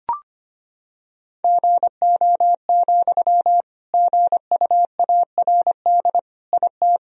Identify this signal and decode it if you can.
Morse code, heard as a keyed tone, about 25 wpm, 700 Hz. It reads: GO, GUARD IT
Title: Sentences from Top 1000 Words - Review Part 4 - 25wpm